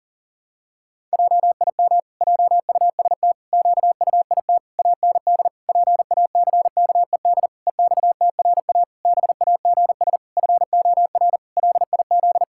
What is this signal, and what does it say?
Morse code, heard as a keyed tone, about 37 wpm, 700 Hz. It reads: JIM JUST QUIT AND PACKED EXTRA BAGS FOR LIZ
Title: Pangram Challenge v2 - 40wpm